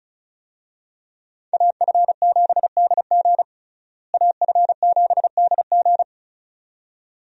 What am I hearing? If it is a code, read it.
Morse code, heard as a keyed tone, about 36 wpm, 700 Hz. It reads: AF7DG AF7DG
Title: USA Call Signs - 5 Characters - Repeated 2x - 35wpm